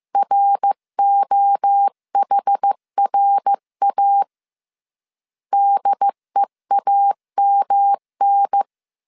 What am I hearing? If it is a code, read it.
Morse code, heard as a keyed tone, about 15 wpm, 800 Hz. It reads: ROHRA DEAMN
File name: CW / DET / 1 3 1500C S A 14wpm 2447